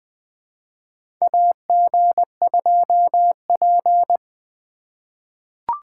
Morse code, heard as a keyed tone, about 20 wpm, 700 Hz. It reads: AG2P
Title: USA Call Signs - 4 Characters - 2x Character Spacing - 20wpm